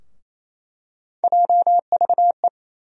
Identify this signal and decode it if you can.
Morse code, heard as a keyed tone, about 28 wpm, 700 Hz. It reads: JVE
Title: USA Call Sign Postfixes - 3 Letters 28wpm